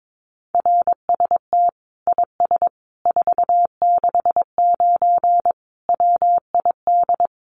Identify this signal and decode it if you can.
Morse code, heard as a keyed tone, about 22 wpm, 700 Hz. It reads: RST IS 469 WID
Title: Common QSO Phrases 22wpm